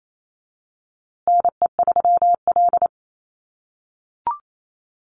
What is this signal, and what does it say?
Morse code, heard as a keyed tone, about 28 wpm, 700 Hz. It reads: NE3L
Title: USA Call Signs Repeated 3x 28wpm